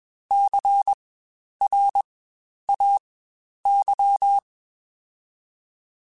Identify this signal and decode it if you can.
Morse code, heard as a keyed tone, about 21 wpm, spaced as for 10 wpm, 800 Hz. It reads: CRAY
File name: Spaced Alphas / Spaced Alphas mp3s / Cs sp @20